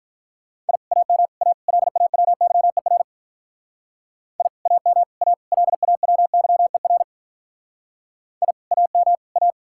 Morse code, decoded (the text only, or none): I AM A LAWYER I AM A LAWYER I AM A